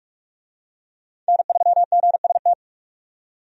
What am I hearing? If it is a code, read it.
Morse code, heard as a keyed tone, about 46 wpm, 700 Hz. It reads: N3GST
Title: USA Call Signs - 5 Characters - 45wpm